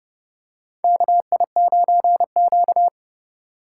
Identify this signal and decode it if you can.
Morse code, heard as a keyed tone, about 30 wpm, 700 Hz. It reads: KI9Q
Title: USA Call Signs - 4 Characters - 4x Character Spacing - 30wpm